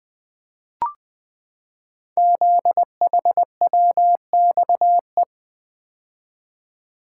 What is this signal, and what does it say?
Morse code, heard as a keyed tone, about 20 wpm, 700 Hz. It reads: ZHWXE
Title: Code Groups - 5 Letters - 20wpm